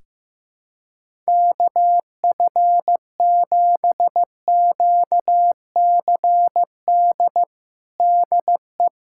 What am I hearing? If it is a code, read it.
Morse code, heard as a keyed tone, about 15 wpm, 700 Hz. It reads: KF7QCD DE
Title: USA Callsign DE Callsign 15wpm